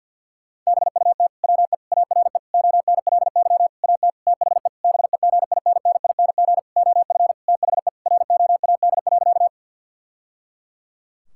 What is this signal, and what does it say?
Morse code, heard as a keyed tone, about 49 wpm, 700 Hz. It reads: BUT WE ARE ONLY AT THE BEGINNING OF THE ROAD.
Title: Sentences from Top 1000 Words - Review Part 5 - 50wpm